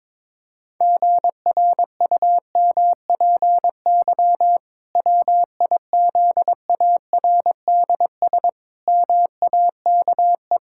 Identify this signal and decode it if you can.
Morse code, heard as a keyed tone, about 22 wpm, 700 Hz. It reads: GRUMPY WIZARDS MAKE
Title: Pangram Challenge v2 - 22wpm